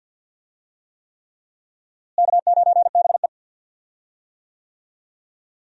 Morse code, heard as a keyed tone, about 50 wpm, 700 Hz. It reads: K9BE
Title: USA Call Signs - 4 Characters - Repeated 3x - 50wpm